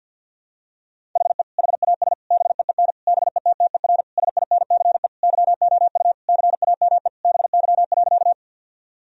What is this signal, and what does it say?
Morse code, heard as a keyed tone, about 50 wpm, 700 Hz. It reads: HE HAS BEEN BETTER SINCE YOU CAME BY.
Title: Sentences from Top 1000 Words - Review Part 1 - 50wpm